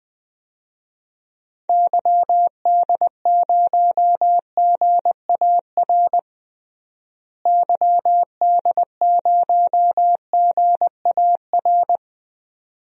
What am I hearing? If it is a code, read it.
Morse code, heard as a keyed tone, about 20 wpm, 700 Hz. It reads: YD0GAR YD0GAR
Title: DXCC-100 Call Signs - 6 Characters - Repeated 2x - 20wpm